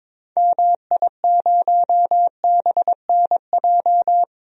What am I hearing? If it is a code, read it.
Morse code, heard as a keyed tone, about 21 wpm, 700 Hz. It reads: MI0BNJ
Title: DXCC-100 Call Signs - 6 Characters - 22wpm